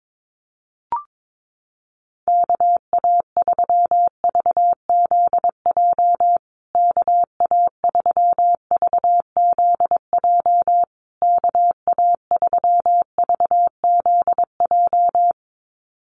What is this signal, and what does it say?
Morse code, heard as a keyed tone, about 22 wpm, 700 Hz. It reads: KA3VZJ KA3VZJ KA3VZJ
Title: USA Call Signs Repeated 3x 22wpm